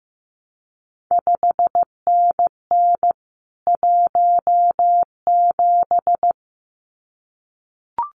Morse code, heard as a keyed tone, about 15 wpm, 700 Hz. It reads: 5NN 17
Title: CQ WW DX Contest Exchange Encore - 15wpm